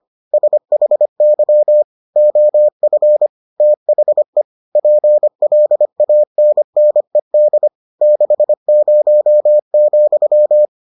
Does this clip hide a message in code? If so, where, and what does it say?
Yes: Morse code from 0.3 to 10.7 s, heard as a keyed tone, about 25 wpm, 600 Hz. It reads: SHY OF THE PLANNED 60,